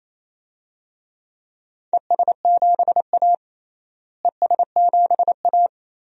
Morse code, heard as a keyed tone, about 28 wpm, 700 Hz. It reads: ES7A ES7A
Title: DXCC-100 Call Signs - 4 Characters - Repeated 3x - 28wpm